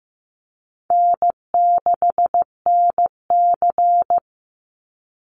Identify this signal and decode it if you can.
Morse code, heard as a keyed tone, about 15 wpm, 700 Hz. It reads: N6NC
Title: Call Signs Repeated 3x - 15wpm